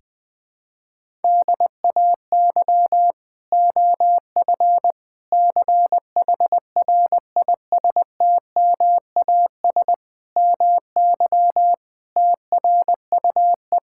Morse code, heard as a keyed tone, about 20 wpm, 700 Hz. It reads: DAY OF CHRISTMAS MY TRUE
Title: Sentences with 9-Word Phrases - Spoken Beginning - 20wpm